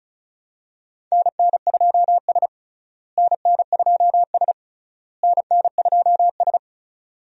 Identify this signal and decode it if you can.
Morse code, heard as a keyed tone, about 36 wpm, 700 Hz. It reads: NN2S NN2S NN2S
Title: USA Call Signs - 4 Characters - Repeated 3x - 35wpm